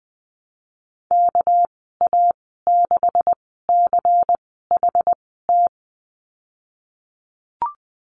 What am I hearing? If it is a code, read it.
Morse code, heard as a keyed tone, about 19 wpm, spaced as for 14 wpm, 700 Hz. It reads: KA6CHT
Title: 231 Lesson 37g - Letter x - Call Signs 20wpm